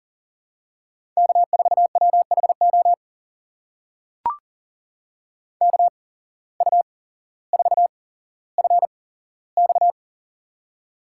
Morse code, heard as a keyed tone, about 39 wpm, spaced as for 17 wpm, 700 Hz. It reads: K4WHO KU4FX
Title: USA Call Signs - 5 Characters - 8x Character Spacing - 40wpm